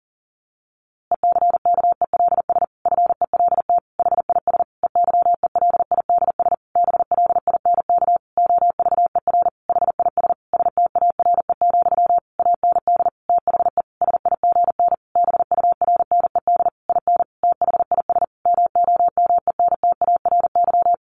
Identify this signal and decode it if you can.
Morse code, heard as a keyed tone, about 37 wpm, 700 Hz. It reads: ECKELS FELT HIS EYELIDS BLINK OVER HIS STARE, AND THE SIGN BURNED IN THIS MOMENTARY
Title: A Sound Of Thunder 40wpm